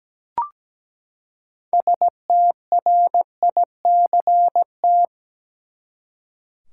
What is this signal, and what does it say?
Morse code, heard as a keyed tone, about 17 wpm, 700 Hz. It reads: STRICT
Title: Six Letter Words - 17wpm